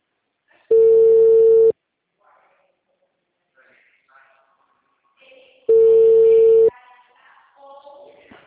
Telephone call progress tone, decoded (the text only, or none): ringback tone